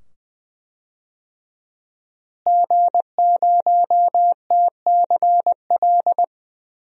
Morse code, heard as a keyed tone, about 20 wpm, 700 Hz. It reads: G0TCL
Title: DXCC-100 Call Signs - 5 Characters - Repeated 2x - 20wpm